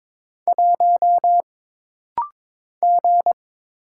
Morse code, heard as a keyed tone, about 22 wpm, 700 Hz. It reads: 1 G
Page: This rapid-fire timing is geared to push your ICR (Instant Character Recognition) skills.